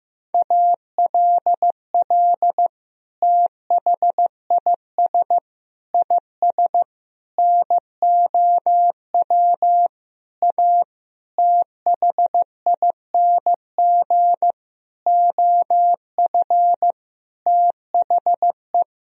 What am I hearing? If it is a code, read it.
Morse code, heard as a keyed tone, about 15 wpm, 700 Hz. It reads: ALL THIS IS NOW A THING OF THE
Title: Sentences from Top 1000 Words - Review Part 3 - 15wpm